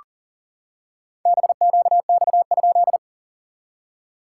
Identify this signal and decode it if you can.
Morse code, heard as a keyed tone, about 40 wpm, 660 Hz. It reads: BQX?
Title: SOTA QSO Practice 40wpm